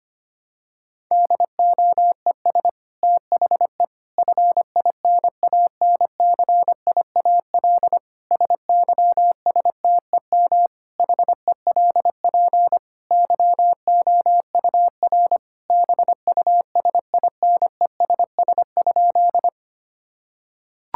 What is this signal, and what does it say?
Morse code, heard as a keyed tone, about 25 wpm, 700 Hz. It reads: DOES THE FINANCIAL SYSTEM HELP YOUR BUSINESS?
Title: Sentences from Top 1000 Words - Review Part 5 - 25wpm